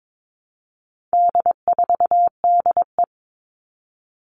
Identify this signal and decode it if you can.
Morse code, heard as a keyed tone, about 22 wpm, 700 Hz. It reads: D4DE
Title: All Cut Numbers - 4-Digit 22wpm